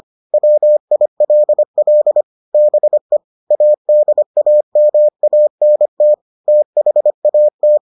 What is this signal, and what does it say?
Morse code, heard as a keyed tone, about 25 wpm, 600 Hz. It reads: WILL BE ADAMANT THAT